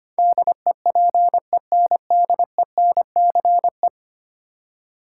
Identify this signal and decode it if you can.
Morse code, heard as a keyed tone, about 25 wpm, 700 Hz. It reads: DEPENDENCE
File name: 10 Letter Words - First 3 Letters Given - 25wpm